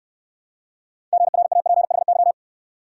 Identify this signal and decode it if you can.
Morse code, heard as a keyed tone, about 66 wpm, 690 Hz. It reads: GRAPHY